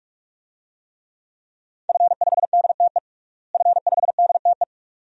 Morse code, heard as a keyed tone, about 46 wpm, 700 Hz. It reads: F5DTE F5DTE
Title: DXCC-100 Call Signs - 5 Characters - Repeated 2x - 45wpm